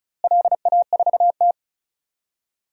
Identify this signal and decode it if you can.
Morse code, heard as a keyed tone, about 36 wpm, 700 Hz. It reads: LA4T
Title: DXCC-100 Call Signs - 4 Characters - Repeated 3x - 35wpm